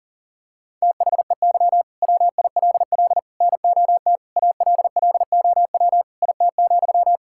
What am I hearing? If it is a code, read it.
Morse code, heard as a keyed tone, about 40 wpm, 700 Hz. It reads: THEY WILL NOT ALLOW IT,
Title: Sentences from Top 1000 Words - Review Part 5 - 40wpm